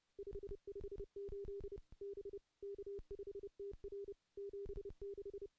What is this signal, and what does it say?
Morse code, heard as a keyed tone, about 30 wpm, 400 Hz. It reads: 558 B K5TR 76
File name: CW-SS-k5tr